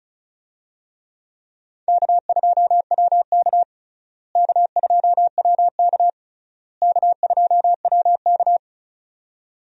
Morse code, heard as a keyed tone, about 34 wpm, 700 Hz. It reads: K2WK K2WK K2WK
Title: USA Call Signs - 4 Characters - Repeated 3x - 35wpm